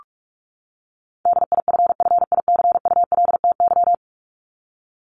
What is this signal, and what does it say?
Morse code, heard as a keyed tone, about 44 wpm, 700 Hz. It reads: DIFFICULTY
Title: Top 1251-1500 Words 45wpm